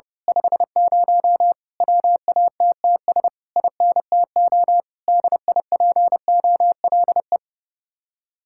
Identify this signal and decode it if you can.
Morse code, heard as a keyed tone, about 29 wpm, 700 Hz. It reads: 50 WATTS INTO DIPOLE
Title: Speed-Racing Common QSO Phrases - 30wpm